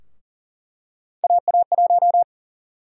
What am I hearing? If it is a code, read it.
Morse code, heard as a keyed tone, about 40 wpm, 700 Hz. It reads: AA1
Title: USA Call Sign Prefixes - 40wpm